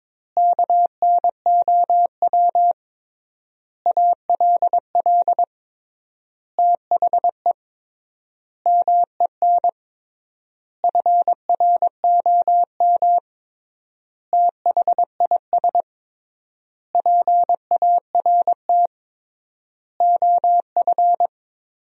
Morse code, heard as a keyed tone, about 22 wpm, 700 Hz. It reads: KNOW ALL THE MEN FROM THIS PART OF
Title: 3x Word-Spacing - Sentences from Top 300 Words - 22wpm